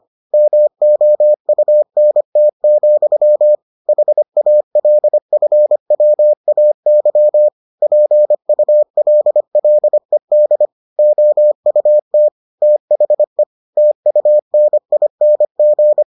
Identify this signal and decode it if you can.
Morse code, heard as a keyed tone, about 25 wpm, 600 Hz. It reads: MOUNT, HALFWAY PULLED OUT THE TUNING